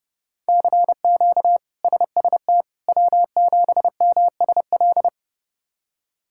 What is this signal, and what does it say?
Morse code, heard as a keyed tone, about 30 wpm, 700 Hz. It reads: CQ SST W7MSL
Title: K1USN SST QSO Practice - 30wpm